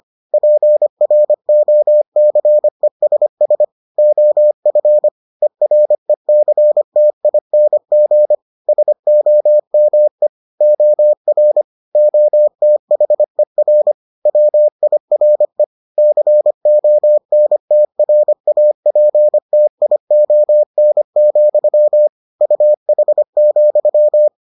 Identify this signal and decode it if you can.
Morse code, heard as a keyed tone, about 25 wpm, 600 Hz. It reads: PROCESS OF ERECTING SOME OR OTHER WIRE CONTRAPTION, UH,